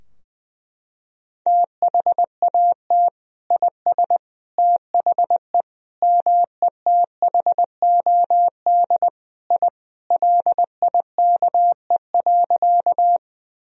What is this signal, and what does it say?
Morse code, heard as a keyed tone, about 20 wpm, 700 Hz. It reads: THAT IS THE METHOD I LIKE.
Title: Sentences from Top 1000 Words - Review Part 3 - 20wpm